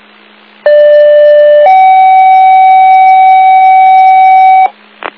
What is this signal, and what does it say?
Two-tone page, heard as a one-second tone, then a three-second tone, about 590 then 760 Hz, with noise. It is heard from 0.7 to 4.7 s.